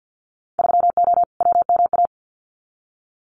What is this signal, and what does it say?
Morse code, heard as a keyed tone, about 50 wpm, 700 Hz. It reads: 3O WMA
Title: Winter Field Day Exchanges Encore 50wpm